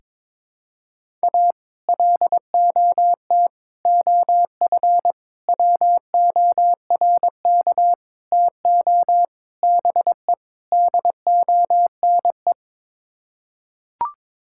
Morse code, heard as a keyed tone, about 22 wpm, 700 Hz. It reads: A LOT OF WORK TO BE DONE
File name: practice / 22wpm / Sets of 7 Words - 22wpm